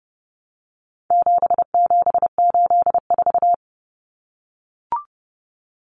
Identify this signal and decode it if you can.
Morse code, heard as a keyed tone, about 30 wpm, 700 Hz. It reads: 7784
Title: Common Cut Numbers - 4-Digit 30wpm